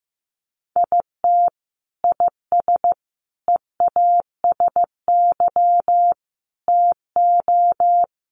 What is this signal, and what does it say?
Morse code, heard as a keyed tone, about 14 wpm, 700 Hz. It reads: IT IS EASY TO
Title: Sentences from Top 1300 Words 15wpm